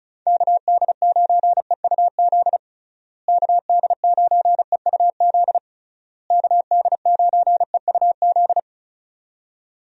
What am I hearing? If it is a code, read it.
Morse code, heard as a keyed tone, about 36 wpm, 700 Hz. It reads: KD9EUZ KD9EUZ KD9EUZ
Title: USA Call Signs - 6 Characters - Repeated 3x - 35wpm